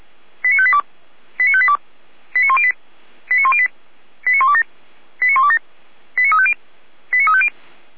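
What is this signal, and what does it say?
Five-tone selective call (ZVEI), heard as quick runs of five tones, tones of about 70 ms. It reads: 89582 89582 89108 89108 89127 89127 89350 89350
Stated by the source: ZVEI five-tone call received by SDR.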